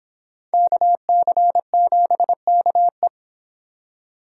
Morse code, heard as a keyed tone, about 26 wpm, 700 Hz. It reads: KC7KE